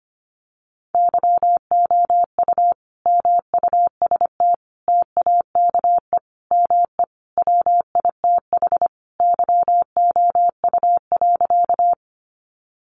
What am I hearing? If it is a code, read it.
Morse code, heard as a keyed tone, about 25 wpm, 700 Hz. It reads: YOU MUST TAKE ME WITH YOU.
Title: Sentences from Top 200 Words 25wpm